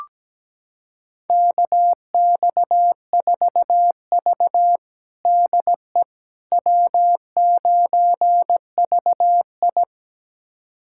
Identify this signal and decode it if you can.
Morse code, heard as a keyed tone, about 17 wpm, 700 Hz. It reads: KX4V DE W9VI
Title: USA Callsign DE Callsign - 4 Characters - 17wpm